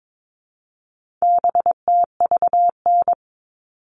Morse code, heard as a keyed tone, about 22 wpm, 700 Hz. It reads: BTVN
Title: All Cut Numbers - 4-Digit 22wpm